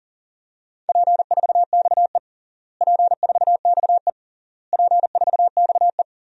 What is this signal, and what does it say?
Morse code, heard as a keyed tone, about 37 wpm, 700 Hz. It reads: P4XE P4XE P4XE
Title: DXCC-100 Call Signs - 4 Characters - Repeated 3x - 40wpm